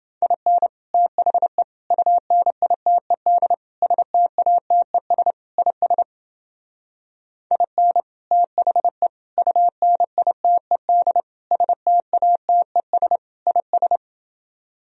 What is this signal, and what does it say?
Morse code, heard as a keyed tone, about 29 wpm, 700 Hz. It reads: IN THE UNITED STATES IS IN THE UNITED STATES IS